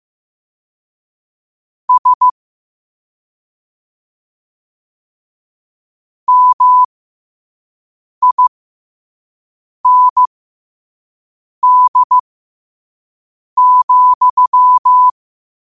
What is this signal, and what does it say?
Morse code, heard as a keyed tone, about 15 wpm, spaced as for 5 wpm, 1000 Hz. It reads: S MIND,
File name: qotdfiles / 5wpm